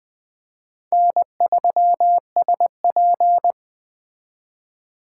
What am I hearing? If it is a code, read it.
Morse code, heard as a keyed tone, about 22 wpm, 700 Hz. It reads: N3SP